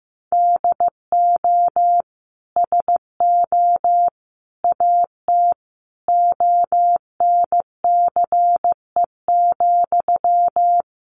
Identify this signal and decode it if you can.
Morse code, heard as a keyed tone, about 15 wpm, 700 Hz. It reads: DO SO AT ONCE,
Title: Sentences from Top 1500 Words 15wpm